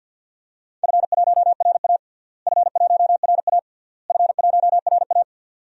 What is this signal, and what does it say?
Morse code, heard as a keyed tone, about 51 wpm, 700 Hz. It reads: F1RA F1RA F1RA
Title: DXCC-100 Call Signs - 4 Characters - Repeated 3x - 50wpm